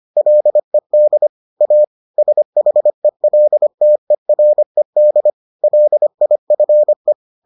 Morse code, heard as a keyed tone, about 25 wpm, 600 Hz. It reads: LED A SHELTERED LIFE